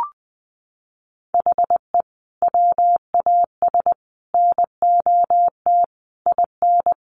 Morse code, heard as a keyed tone, about 20 wpm, 700 Hz. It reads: HE WAS NOT IN
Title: Sentences from Top 1300 Words 20wpm